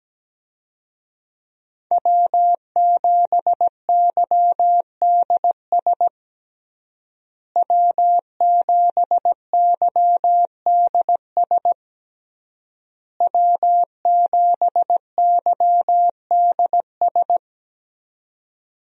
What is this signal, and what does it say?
Morse code, heard as a keyed tone, about 17 wpm, 700 Hz. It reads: W7YDS W7YDS W7YDS
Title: USA Call Signs - 5 Characters - Repeated 3x - 17wpm